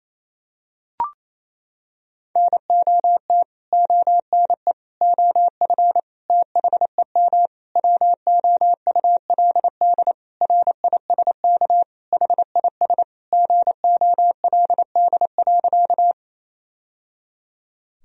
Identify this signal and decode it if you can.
Morse code, heard as a keyed tone, about 28 wpm, 700 Hz. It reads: NOT ONE OF THEM WOULD RISK HIS GOLD.
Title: Sentences from Top 1000 Words - Review Part 4 - 28wpm